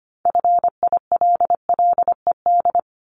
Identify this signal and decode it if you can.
Morse code, heard as a keyed tone, about 25 wpm, 700 Hz. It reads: FILLED
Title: Multi-letter Phonemes 25wpm